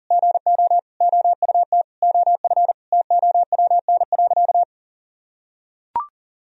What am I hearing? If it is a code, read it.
Morse code, heard as a keyed tone, about 37 wpm, 700 Hz. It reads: GO OUT OF TOWN.
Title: Sentences from Top 1000 Words - Review Part 4 - 40wpm